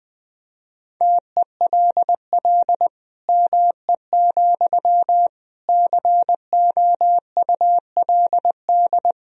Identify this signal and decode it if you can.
Morse code, heard as a keyed tone, about 20 wpm, 700 Hz. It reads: TELL ME, COULD
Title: Sentences from Top 1000 Words - Review Part 2 - 20wpm